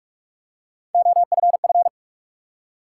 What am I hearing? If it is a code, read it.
Morse code, heard as a keyed tone, about 48 wpm, 700 Hz. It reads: OFF
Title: Three Letter Words - 45wpm